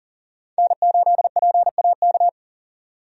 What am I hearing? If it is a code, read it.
Morse code, heard as a keyed tone, about 37 wpm, 700 Hz. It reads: N8PAK
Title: USA Call Signs - 5 Characters - 40wpm